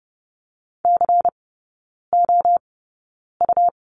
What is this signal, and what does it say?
Morse code, heard as a keyed tone, about 30 wpm, spaced as for 9 wpm, 700 Hz. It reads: COU
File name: Lesson 18e - Letter m - Three Characters 30wpm